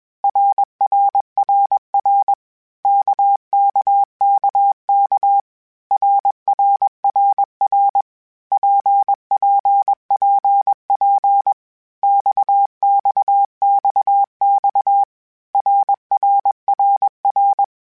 Morse code, heard as a keyed tone, about 21 wpm, 800 Hz. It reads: RRRR KKKK RRRR PPPP XXXX RRRR
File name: Groups@20 / Groups@20 mp3s / RKPX@20